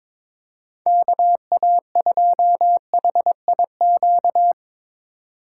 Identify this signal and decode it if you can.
Morse code, heard as a keyed tone, about 22 wpm, 700 Hz. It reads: KA2HIQ
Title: USA Call Signs - 6 Characters - 2x Character Spacing - 22wpm